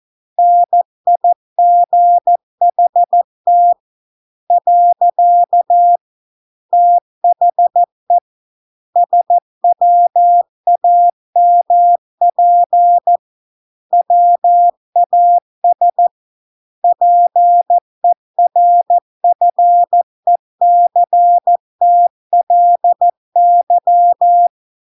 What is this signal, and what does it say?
Morse code, heard as a keyed tone, about 14 wpm, 700 Hz. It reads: NIGHT . THE SWAMP WAS PERFECTLY